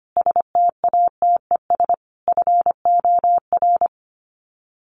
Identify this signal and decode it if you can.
Morse code, heard as a keyed tone, about 25 wpm, 700 Hz. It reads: STATES FOR
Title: Sets of 5 Words Encore2 25wpm